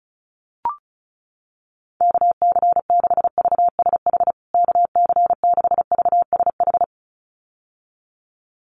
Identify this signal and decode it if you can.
Morse code, heard as a keyed tone, about 34 wpm, 700 Hz. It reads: KC6VSH KC6VSH
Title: USA Call Signs Repeated 2x 35wpm